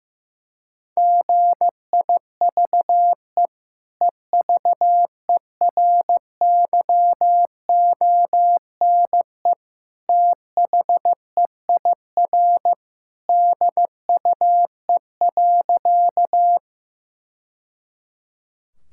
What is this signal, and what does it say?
Morse code, heard as a keyed tone, about 16 wpm, 700 Hz. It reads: GIVE EVERYONE THEIR DUE.
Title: Sentences from Top 1000 Words - Review Part 6 - 15wpm